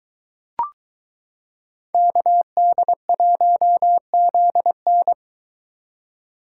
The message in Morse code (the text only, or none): KD1ZN